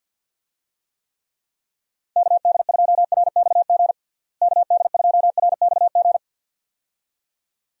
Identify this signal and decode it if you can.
Morse code, heard as a keyed tone, about 51 wpm, 700 Hz. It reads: KD2RXG KD2RXG
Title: USA Call Signs - 6 Characters - Repeated 2x - 50wpm